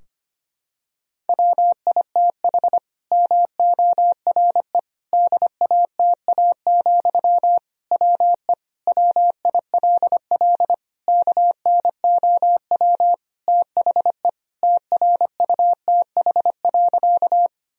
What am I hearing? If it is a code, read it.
Morse code, heard as a keyed tone, about 25 wpm, 700 Hz. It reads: WITH MORE DATA, WE WILL KNOW THE TRUTH.
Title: Sentences from Top 1000 Words - Review Part 3 - 25wpm